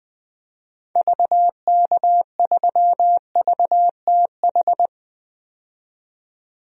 Morse code, heard as a keyed tone, about 20 wpm, 700 Hz. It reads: VK3VTH
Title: DXCC-100 Call Signs - 6 Characters - 20wpm